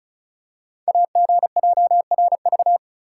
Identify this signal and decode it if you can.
Morse code, heard as a keyed tone, about 36 wpm, 700 Hz. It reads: AGJRV
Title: Code Groups - 5 Letter:Numbers - 35wpm